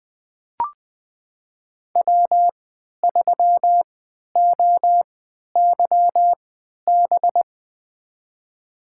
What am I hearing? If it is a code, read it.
Morse code, heard as a keyed tone, about 20 wpm, spaced as for 11 wpm, 700 Hz. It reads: W3OYB
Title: USA Call Signs - 5 Characters - 3x Character Spacing - 20wpm